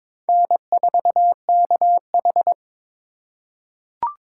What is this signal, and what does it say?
Morse code, heard as a keyed tone, about 22 wpm, 700 Hz. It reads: N4KH
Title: K1USN SST QSO Practice - 22wpm